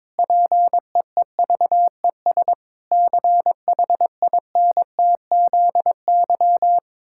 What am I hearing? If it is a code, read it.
Morse code, heard as a keyed tone, about 22 wpm, 700 Hz. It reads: PEEVES CHINTZY
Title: Pangram Challenge v2 - 22wpm